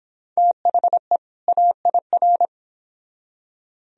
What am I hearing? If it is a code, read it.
Morse code, heard as a keyed tone, about 26 wpm, 700 Hz. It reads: THE AIR